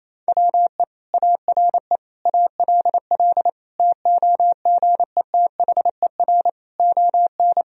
Morse code, heard as a keyed tone, about 28 wpm, 700 Hz. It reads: WE ARE ALL TOGETHER ON
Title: Sentences from Top 1000 Words - Review Part 2 - 28wpm